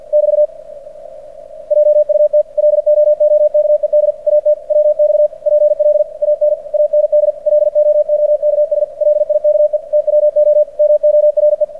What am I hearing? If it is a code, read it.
Morse code, heard as a keyed tone, about 46 wpm, 600 Hz. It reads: <BT> JUT LOOKED AT WX PLAN AND LOOKS LIKE TWO MORE
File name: archived / CRSnet